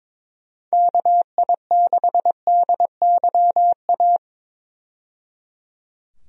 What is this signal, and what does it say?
Morse code, heard as a keyed tone, about 21 wpm, 700 Hz. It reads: KI6DYA
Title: USA Call Signs - 6 Characters - 22wpm